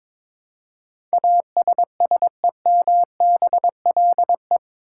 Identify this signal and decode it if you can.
Morse code, heard as a keyed tone, about 22 wpm, 700 Hz. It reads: ASSEMBLE
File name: practice / 22wpm / Eight Letter Words - 22wpm